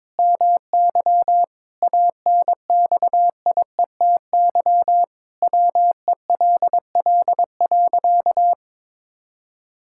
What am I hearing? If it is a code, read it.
Morse code, heard as a keyed tone, about 21 wpm, 700 Hz. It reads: MY ANXIETY WELL.
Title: Sentences from Top 1600 Words 22wpm